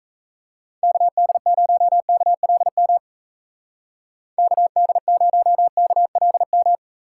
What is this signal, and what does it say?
Morse code, heard as a keyed tone, about 33 wpm, 700 Hz. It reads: KD0KLM KD0KLM
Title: Speed Racing - USA 2x3 Call Signs 28wpm